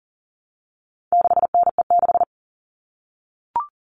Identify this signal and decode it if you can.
Morse code, heard as a keyed tone, about 40 wpm, 700 Hz. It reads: 6NE6
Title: All Cut Numbers - 4-Digit 40wpm